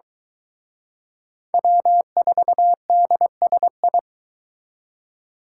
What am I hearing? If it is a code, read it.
Morse code, heard as a keyed tone, about 22 wpm, 700 Hz. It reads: W4DSI